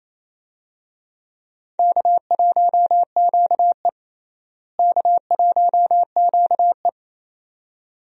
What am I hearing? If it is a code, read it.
Morse code, heard as a keyed tone, about 28 wpm, 700 Hz. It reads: K1QE K1QE
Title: USA Call Signs - 4 Characters - Repeated 2x - 28wpm